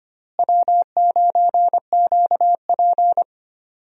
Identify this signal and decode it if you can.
Morse code, heard as a keyed tone, about 25 wpm, 700 Hz. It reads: W9QP
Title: USA Call Signs - 4 Characters - Repeated 3x - 25wpm